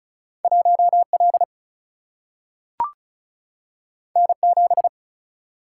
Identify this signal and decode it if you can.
Morse code, heard as a keyed tone, about 36 wpm, 700 Hz. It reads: 1L N7
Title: Single-Letter Single-Number 35wpm